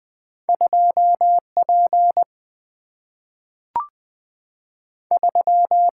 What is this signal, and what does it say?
Morse code, heard as a keyed tone, about 20 wpm, 700 Hz. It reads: 2P 3
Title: Single-Letter Single-Number 20wpm